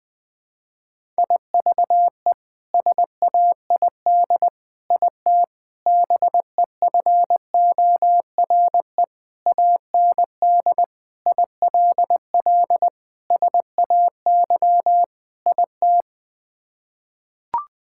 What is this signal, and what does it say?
Morse code, heard as a keyed tone, about 21 wpm, 700 Hz. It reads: IVE SAID IT BEFORE AND ILL SAY IT
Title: Sets of 8 Words - 20wpm